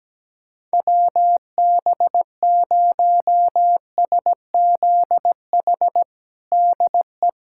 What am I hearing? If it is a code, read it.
Morse code, heard as a keyed tone, about 17 wpm, 700 Hz. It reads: WB0SZH DE
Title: USA Callsign DE Callsign 17wpm